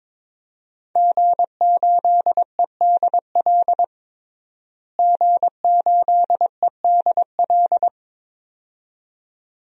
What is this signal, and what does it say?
Morse code, heard as a keyed tone, about 22 wpm, 700 Hz. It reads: G8EDL G8EDL
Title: DXCC-100 Call Signs - 5 Characters - Repeated 2x - 22wpm